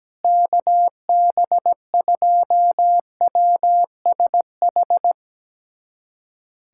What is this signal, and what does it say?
Morse code, heard as a keyed tone, about 17 wpm, 700 Hz. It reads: KB2WSH
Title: USA Callsign DE Callsign 17wpm